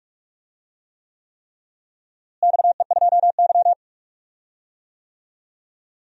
Morse code, heard as a keyed tone, about 46 wpm, 700 Hz. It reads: XE2Y
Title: DXCC-100 Call Signs - 4 Characters - Repeated 3x - 45wpm